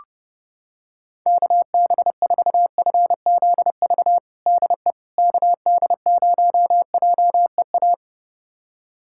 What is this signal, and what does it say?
Morse code, heard as a keyed tone, about 31 wpm, 700 Hz. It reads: KB4FZV DE KD0JEA
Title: USA Callsign DE Callsign - 6 Characters - 30wpm